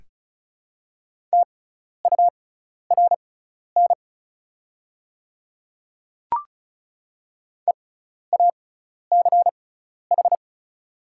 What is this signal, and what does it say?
Morse code, heard as a keyed tone, about 36 wpm, spaced as for 12 wpm, 700 Hz. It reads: TURN EACH
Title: 6x Character Spacing - Top 300 Words - 35wpm